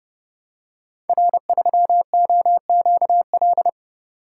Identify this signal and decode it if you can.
Morse code, heard as a keyed tone, about 30 wpm, 700 Hz. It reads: R3OQL
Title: Code Groups - 5 Letter:Numbers - 30wpm